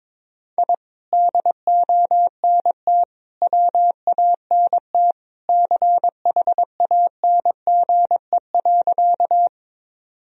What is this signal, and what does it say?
Morse code, heard as a keyed tone, about 21 wpm, 700 Hz. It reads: I DONT WANT CHANGE.
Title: Sentences from Top 1000 Words - Review Part 1 - 22wpm